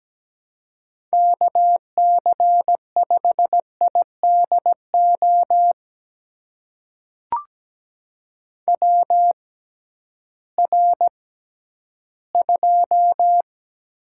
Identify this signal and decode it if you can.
Morse code, heard as a keyed tone, about 17 wpm, spaced as for 11 wpm, 700 Hz. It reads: KC5IDO WR2
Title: USA Call Signs - 6x Character Spacing - 17wpm